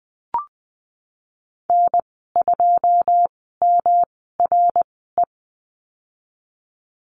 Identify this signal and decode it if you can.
Morse code, heard as a keyed tone, about 20 wpm, spaced as for 14 wpm, 700 Hz. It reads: N2MRE
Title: 193 Lesson 31g - Letter v - Call Signs 20wpm